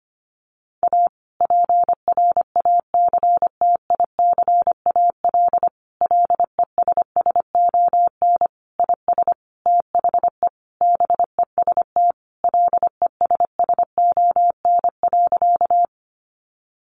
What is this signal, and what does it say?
Morse code, heard as a keyed tone, about 25 wpm, 700 Hz. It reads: A PRACTICAL LESSON IS THE BEST LESSON.
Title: Sentences from Top 1400 Words 25wpm